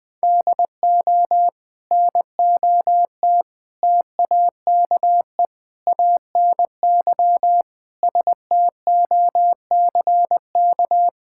Morse code, heard as a keyed tone, about 21 wpm, 700 Hz. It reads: DO NOT TAKE ANY STOCK
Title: Sentences from Top 1000 Words - Review Part 5 - 20wpm